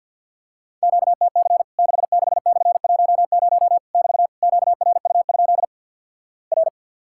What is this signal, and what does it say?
Morse code, heard as a keyed tone, about 50 wpm, 700 Hz. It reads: QTC 66/10 <BT> QRU? R
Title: Worked All Europe DX Contest - QTC Exchange with Repeats - 50wpm